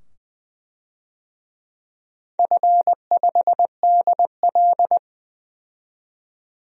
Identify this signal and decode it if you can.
Morse code, heard as a keyed tone, about 21 wpm, 700 Hz. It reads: F5DL
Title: DXCC-100 Call Signs - 4 Characters - 20wpm